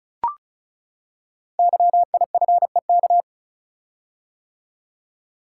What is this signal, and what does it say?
Morse code, heard as a keyed tone, about 34 wpm, 690 Hz. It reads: YIFEK
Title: Code Groups - 5 Letters - 35wpm